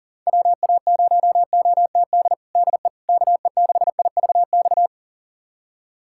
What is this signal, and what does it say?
Morse code, heard as a keyed tone, about 40 wpm, 700 Hz. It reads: WA0OTD DE KE6IVX
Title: USA Callsign DE Callsign 40wpm